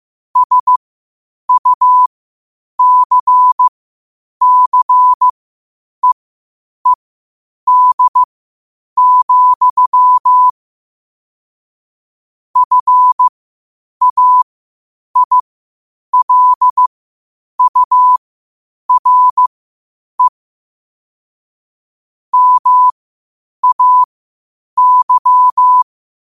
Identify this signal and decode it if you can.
Morse code, heard as a keyed tone, about 15 wpm, spaced as for 8 wpm, 1000 Hz. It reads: SUCCEED, FAILURE MAY
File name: qotdfiles / 8wpm